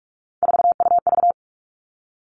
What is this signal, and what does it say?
Morse code, heard as a keyed tone, about 46 wpm, 700 Hz. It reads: 4UV
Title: Three Challenging Letter-Number Combinations 45wpm